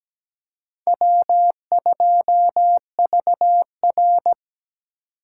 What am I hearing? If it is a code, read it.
Morse code, heard as a keyed tone, about 17 wpm, 700 Hz. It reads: W2VR
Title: USA Call Signs - 4 Characters - 4x Character Spacing - 17wpm